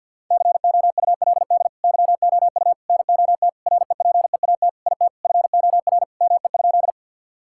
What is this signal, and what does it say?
Morse code, heard as a keyed tone, about 50 wpm, 700 Hz. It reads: COULD YOU NOT REPEAT IT FOR ME?
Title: Sentences from Top 1600 Words 50wpm